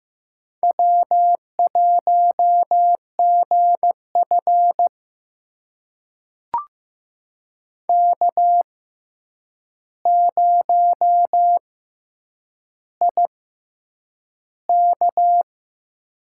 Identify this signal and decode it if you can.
Morse code, heard as a keyed tone, about 15 wpm, spaced as for 8 wpm, 700 Hz. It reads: W1GF K0IK
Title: USA Call Signs - 4 Characters - 6x Character Spacing - 15wpm